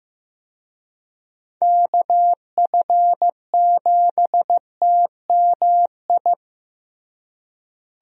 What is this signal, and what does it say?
Morse code, heard as a keyed tone, about 16 wpm, 700 Hz. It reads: KF7TMI
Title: USA Call Signs - 6 Characters - Repeated 3x - 15wpm